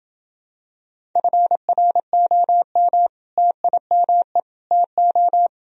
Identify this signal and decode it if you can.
Morse code, heard as a keyed tone, about 27 wpm, 700 Hz. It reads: FROM TIME TO